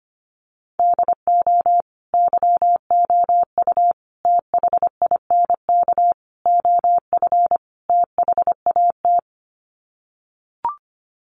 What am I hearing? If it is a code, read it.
Morse code, heard as a keyed tone, about 25 wpm, 700 Hz. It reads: DO YOU THINK OF THAT
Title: Sets of 5 Words Encore2 25wpm